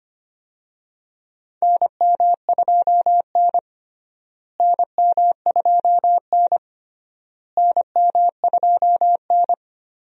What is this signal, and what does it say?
Morse code, heard as a keyed tone, about 25 wpm, 700 Hz. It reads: NM2N NM2N NM2N
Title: USA Call Signs - 4 Characters - Repeated 3x - 25wpm